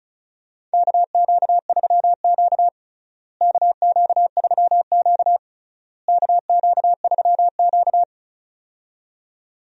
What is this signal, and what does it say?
Morse code, heard as a keyed tone, about 34 wpm, 700 Hz. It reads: KQ3Q KQ3Q KQ3Q
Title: USA Call Signs - 4 Characters - Repeated 3x - 35wpm